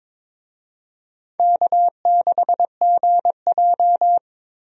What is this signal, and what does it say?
Morse code, heard as a keyed tone, about 22 wpm, 700 Hz. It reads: K6GJ
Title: USA Call Signs - 4 Characters - 22wpm